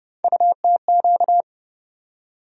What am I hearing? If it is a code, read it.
Morse code, heard as a keyed tone, about 30 wpm, 700 Hz. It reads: UTQ
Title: USA Call Sign Postfixes - 3 Letters 30wpm